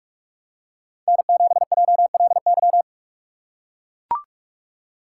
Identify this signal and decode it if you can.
Morse code, heard as a keyed tone, about 44 wpm, 700 Hz. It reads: N7JLY
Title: USA Call Signs - 5 Characters - 6x Character Spacing - 45wpm